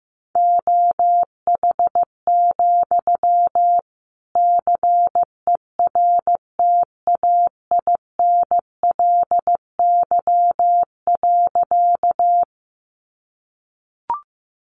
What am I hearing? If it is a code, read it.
Morse code, heard as a keyed tone, about 15 wpm, 700 Hz. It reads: OH, CERTAINLY.
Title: Sentences from Top 900 Words 15wpm